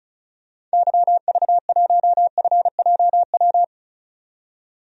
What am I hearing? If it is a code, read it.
Morse code, heard as a keyed tone, about 36 wpm, 700 Hz. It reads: YV1FJW
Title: DXCC-100 Call Signs - 6 Characters - 35wpm